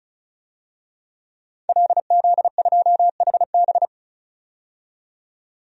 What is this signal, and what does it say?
Morse code, heard as a keyed tone, about 34 wpm, 700 Hz. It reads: LZ2HB
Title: DXCC-100 Call Signs - 5 Characters - 35wpm